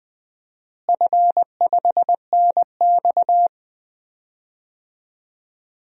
Morse code, heard as a keyed tone, about 19 wpm, 700 Hz. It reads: F5NX
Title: DXCC-100 Call Signs - 4 Characters - Repeated 3x - 20wpm